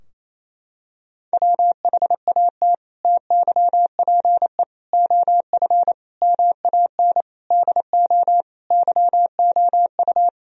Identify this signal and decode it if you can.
Morse code, heard as a keyed tone, about 27 wpm, 700 Hz. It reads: WHAT TYPE OF MAN DO YOU
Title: Sentences from Top 1000 Words - Review Part 3 - 28wpm